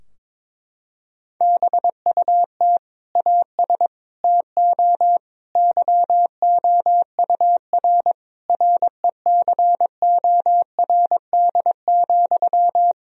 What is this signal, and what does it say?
Morse code, heard as a keyed tone, about 22 wpm, 700 Hz. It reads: BUT AS TO YOUR RECORD,
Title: Sentences from Top 1000 Words - Review Part 3 - 22wpm